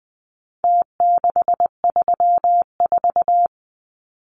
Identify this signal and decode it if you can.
Morse code, heard as a keyed tone, about 20 wpm, 700 Hz. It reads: T634
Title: Numbers - Contest Style - 4-Digit 20wpm